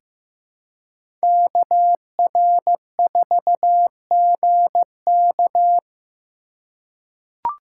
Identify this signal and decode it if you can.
Morse code, heard as a keyed tone, about 15 wpm, 700 Hz. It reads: KR4GK
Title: USA Call Signs - 5 Characters - 4x Character Spacing - 15wpm